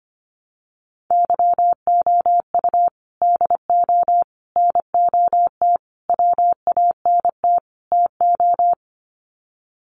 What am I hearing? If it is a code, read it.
Morse code, heard as a keyed tone, about 25 wpm, 700 Hz. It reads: YOU DO NOT WANT TO
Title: Sets of 5 Words 25wpm